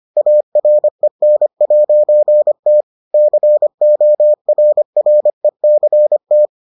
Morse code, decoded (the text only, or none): AREN'T CORRECT